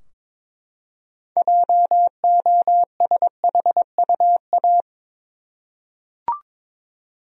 Morse code, heard as a keyed tone, about 22 wpm, 700 Hz. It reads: JOSHUA
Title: USA Common Names - 22wpm